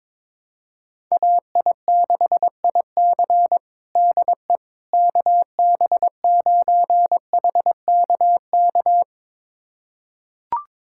Morse code, heard as a keyed tone, about 22 wpm, 700 Hz. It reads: AI6IC DE KB9HKK
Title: USA Callsign DE Callsign 22wpm